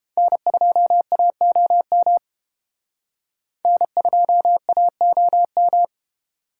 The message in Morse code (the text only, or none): N2AOM N2AOM